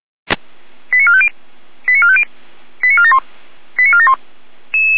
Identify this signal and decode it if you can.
Five-tone selective call (ZVEI), heard as quick runs of five tones, tones of about 70 ms. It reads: 89450 89450 89471 89471